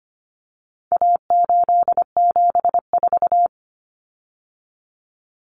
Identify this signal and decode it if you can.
Morse code, heard as a keyed tone, about 25 wpm, 700 Hz. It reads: A874
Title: Common Cut Numbers - 4-Digit 25wpm